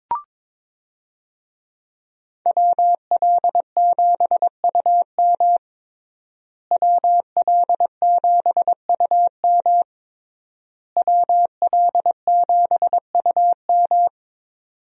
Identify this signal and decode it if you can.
Morse code, heard as a keyed tone, about 22 wpm, 700 Hz. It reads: WL7UM WL7UM WL7UM
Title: USA Call Signs - 5 Characters - Repeated 3x - 22wpm